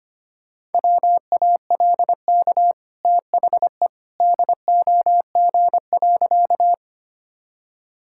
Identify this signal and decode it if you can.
Morse code, heard as a keyed tone, about 25 wpm, 700 Hz. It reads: WALK THE DOG.
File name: Easy Sentences with Hints - 25wpm